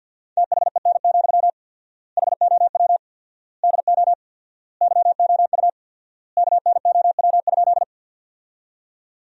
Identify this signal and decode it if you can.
Morse code, heard as a keyed tone, about 50 wpm, 700 Hz. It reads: THEN, HOW DO YOU KNOW?
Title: 4x Word-Spacing - Sentences from Top 100 Words - 50wpm